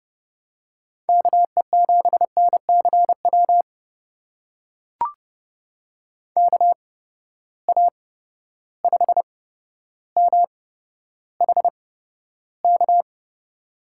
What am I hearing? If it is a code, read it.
Morse code, heard as a keyed tone, about 31 wpm, spaced as for 13 wpm, 700 Hz. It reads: KE7NCW KA5MHK
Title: USA Call Signs - 6 Characters - 8x Character Spacing - 30wpm